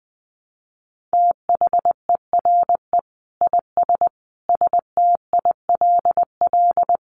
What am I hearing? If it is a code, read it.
Morse code, heard as a keyed tone, about 20 wpm, 700 Hz. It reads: THERE IS STILL
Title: Sentences from Top 1000 Words 20wpm